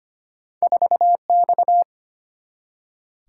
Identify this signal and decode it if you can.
Morse code, heard as a keyed tone, about 25 wpm, 700 Hz. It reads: 4X
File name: Two Letter-Number Combinations - 25wpm